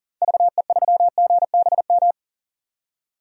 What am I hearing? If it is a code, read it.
Morse code, heard as a keyed tone, about 40 wpm, 700 Hz. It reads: VE3GBM
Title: DXCC-100 Call Signs - 6 Characters - 40wpm